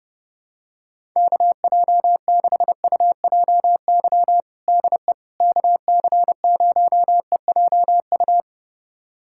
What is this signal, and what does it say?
Morse code, heard as a keyed tone, about 30 wpm, 700 Hz. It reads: KJ6UJY DE KC0EJU
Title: USA Callsign DE Callsign - 6 Characters - 30wpm